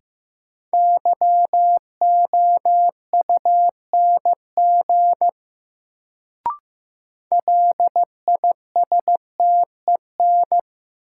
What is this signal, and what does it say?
Morse code, heard as a keyed tone, about 15 wpm, 700 Hz. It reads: YOUNG LISTEN
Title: Sentences from Top 400 Words - New Word Drill - 15wpm